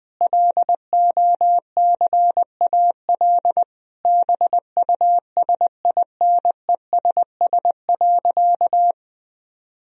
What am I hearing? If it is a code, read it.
Morse code, heard as a keyed tone, about 20 wpm, 700 Hz. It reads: LOCAL BUSINESS.
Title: Sentences from Top 1000 Words - Review Part 3 - 20wpm